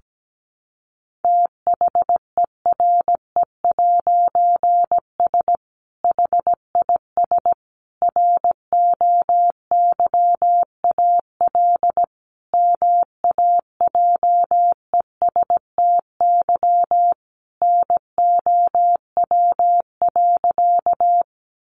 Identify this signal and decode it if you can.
Morse code, heard as a keyed tone, about 17 wpm, 700 Hz. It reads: THERE'S HIS ROYAL MAJESTY NOW.
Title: A Sound Of Thunder 17wpm